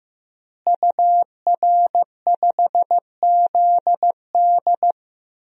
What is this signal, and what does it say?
Morse code, heard as a keyed tone, about 16 wpm, 700 Hz. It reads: UR5ZD
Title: DXCC-100 Call Signs - 5 Characters - Repeated 2x - 15wpm